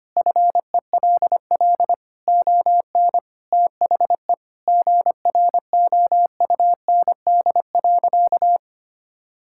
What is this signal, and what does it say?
Morse code, heard as a keyed tone, about 25 wpm, 700 Hz. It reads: FELL ON THE GROUND.
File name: practice / 25wpm / Sentences with 3-Word Phrases - 25wpm